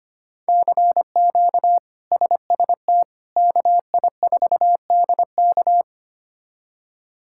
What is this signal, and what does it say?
Morse code, heard as a keyed tone, about 25 wpm, 700 Hz. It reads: CQ SST KI4DK
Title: K1USN SST QSO Practice - 25wpm